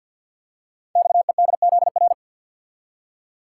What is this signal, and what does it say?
Morse code, heard as a keyed tone, about 52 wpm, 700 Hz. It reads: XEDZR
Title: Code Groups - 5 Letters - 50wpm